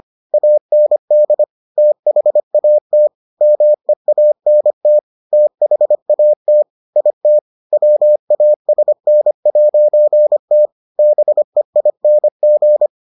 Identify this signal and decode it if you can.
Morse code, heard as a keyed tone, about 25 wpm, 600 Hz. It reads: AND THAT MEANT THAT IT WASN'T BEING